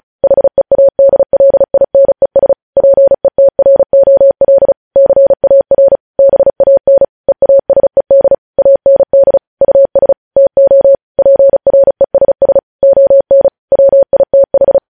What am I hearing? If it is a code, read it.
Morse code, heard as a keyed tone, about 36 wpm, 550 Hz. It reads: HEADLINES PETROL CAR BAN EASED AND US TO PRESS ON WITH
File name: podcast / News headlines-1x-35x35